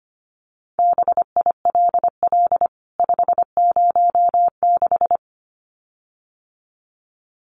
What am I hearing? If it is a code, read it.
Morse code, heard as a keyed tone, about 25 wpm, 700 Hz. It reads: BILL 506
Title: CWT Exchange 25wpm